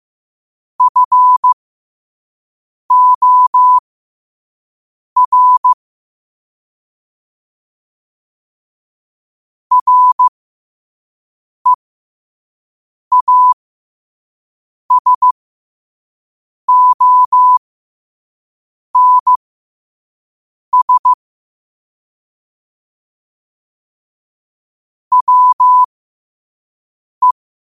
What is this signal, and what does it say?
Morse code, heard as a keyed tone, about 15 wpm, spaced as for 5 wpm, 1000 Hz. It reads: FOR REASONS WE
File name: qotdfiles / 5wpm